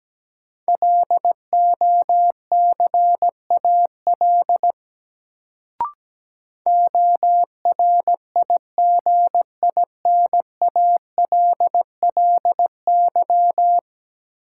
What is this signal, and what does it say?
Morse code, heard as a keyed tone, about 17 wpm, 700 Hz. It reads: LOCAL ORIGINALLY
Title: Sentences from Top 1500 Words - New Word Drill Alt - 17wpm